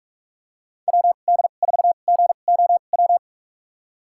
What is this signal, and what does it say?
Morse code, heard as a keyed tone, about 45 wpm, spaced as for 32 wpm, 700 Hz. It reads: WD4GOW
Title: USA Call Signs - 6 Characters - 2x Character Spacing - 45wpm